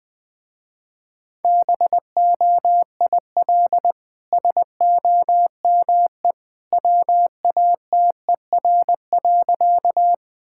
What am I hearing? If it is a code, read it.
Morse code, heard as a keyed tone, about 20 wpm, 700 Hz. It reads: BOIL SOME WATER.
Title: Easy Sentences with Hints - 20wpm